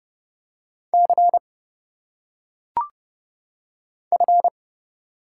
Morse code, heard as a keyed tone, about 30 wpm, 700 Hz. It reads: C F
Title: Single Letters - Phonetic Pronunciation 30wpm